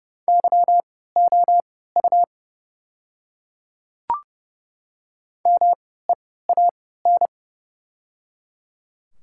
Morse code, heard as a keyed tone, about 30 wpm, spaced as for 17 wpm, 700 Hz. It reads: YOU MEAN
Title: 3x Character Spacing - Top 200 Words - 30wpm